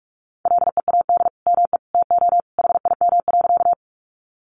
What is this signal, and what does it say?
Morse code, heard as a keyed tone, about 44 wpm, 700 Hz. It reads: LEAD ME TO HIM.
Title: Sentences from Top 500 Words 45wpm